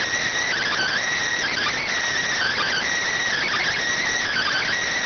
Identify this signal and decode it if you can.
Emergency Alert System (SAME) tones (header bursts), heard as harsh buzzing bursts. It is heard from 0.4 to 4.8 s.